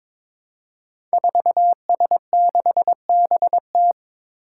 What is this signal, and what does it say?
Morse code, heard as a keyed tone, about 22 wpm, 700 Hz. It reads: 4S6BT
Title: Five Challenging Letter-Number Combinations 22wpm